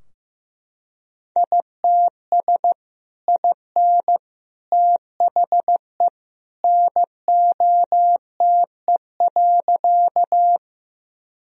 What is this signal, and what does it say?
Morse code, heard as a keyed tone, about 15 wpm, 700 Hz. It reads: ITS IN THE NOTE.
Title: Sentences from Top 1000 Words - Review Part 2 - 15wpm